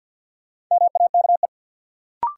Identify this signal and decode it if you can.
Morse code, heard as a keyed tone, about 46 wpm, 700 Hz. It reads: MAKE
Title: Sentences from Top 100 Words - New Word Drill - 50wpm